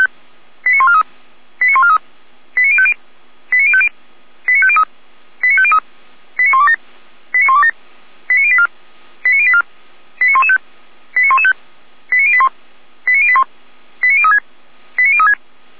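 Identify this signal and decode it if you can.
Five-tone selective call (ZVEI), heard as quick runs of five tones, tones of about 70 ms. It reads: 89143 89143 89050 89050 89503 89503 89127 89127 89084 89084 89115 89115 89081 89081 89036 89036
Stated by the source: ZVEI five-tone call received by SDR, from the feuerwehr-orlamuende.org fire brigade server.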